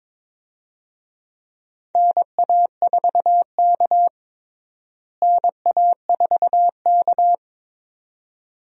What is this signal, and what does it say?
Morse code, heard as a keyed tone, about 22 wpm, 700 Hz. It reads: NA4K NA4K
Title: USA Call Signs - 4 Characters - Repeated 2x - 22wpm